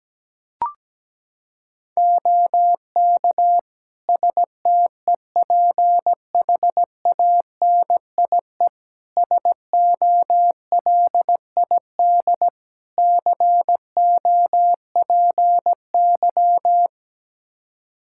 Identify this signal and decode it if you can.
Morse code, heard as a keyed tone, about 17 wpm, 700 Hz. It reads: OK STEPHANIE SOLID COPY
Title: Practice QSO - Exchange 3 - More Info - 17wpm